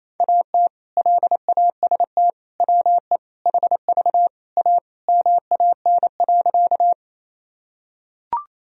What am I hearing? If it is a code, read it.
Morse code, heard as a keyed tone, about 28 wpm, 700 Hz. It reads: AT LAST WE HV A MAN.
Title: Sentences from Top 300 Words and Abbreviations 28wpm